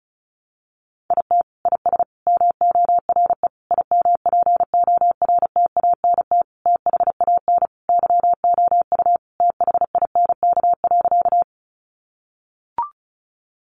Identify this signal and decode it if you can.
Morse code, heard as a keyed tone, about 34 wpm, 700 Hz. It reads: IT IS MORE IMPORTANT THAN YOU THINK.
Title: Sentences from Top 600 Words 35wpm